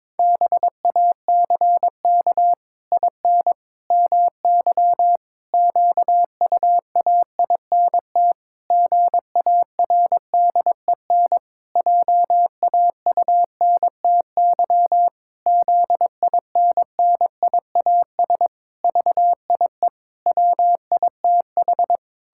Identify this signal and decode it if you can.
Morse code, heard as a keyed tone, about 22 wpm, 700 Hz. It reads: BACK IN MY QUAINT GARDEN JAUNTY ZINNIAS VIE WITH
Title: Pangram Challenge v2 - 22wpm